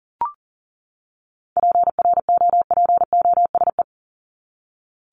Morse code, heard as a keyed tone, about 40 wpm, 700 Hz. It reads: PROPOSE
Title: ED Suffix - Top 100 Words with Base Words 40wpm